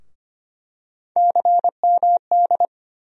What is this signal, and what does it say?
Morse code, heard as a keyed tone, about 25 wpm, 700 Hz. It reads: CMD
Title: USA Call Sign Postfixes - 3 Letters 25wpm